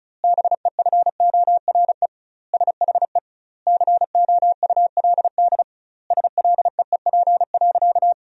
Morse code, heard as a keyed tone, about 36 wpm, 700 Hz. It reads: BEFORE SHE COULD SLEEP.
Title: 2x Word-Spacing - Sentences from Top 500 Words - 35wpm